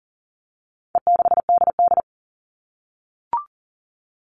Morse code, heard as a keyed tone, about 40 wpm, 700 Hz. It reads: E6DD
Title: All Cut Numbers - 4-Digit 40wpm